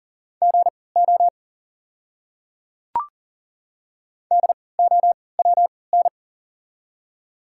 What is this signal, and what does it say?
Morse code, heard as a keyed tone, about 40 wpm, spaced as for 23 wpm, 700 Hz. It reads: GO DOWN
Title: 3x Character Spacing - Top 100 Words - 40wpm